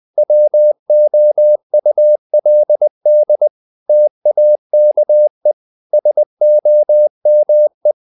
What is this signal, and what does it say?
Morse code, heard as a keyed tone, about 20 wpm, 600 Hz. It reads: WOULD TAKE SOME
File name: the old man and the sea part 2 20wpm